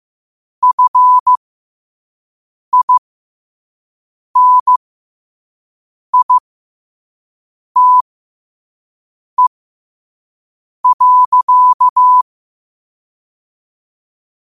Morse code, heard as a keyed tone, about 15 wpm, spaced as for 5 wpm, 1000 Hz. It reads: FINITE.
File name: qotdfiles / 5wpm